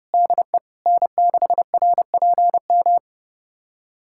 Morse code, heard as a keyed tone, about 30 wpm, 700 Hz. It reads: DE N6RPM
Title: USA Callsign DE Callsign - 5 Characters - 30wpm